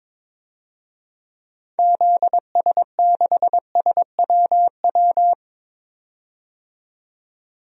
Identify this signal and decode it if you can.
Morse code, heard as a keyed tone, about 22 wpm, 700 Hz. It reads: ZS6SWW
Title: DXCC-100 Call Signs - 6 Characters - 22wpm